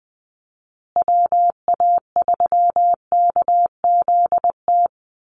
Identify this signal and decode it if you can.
Morse code, heard as a keyed tone, about 20 wpm, 700 Hz. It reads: WA3KZT
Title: USA Call Signs Repeated 3x 20wpm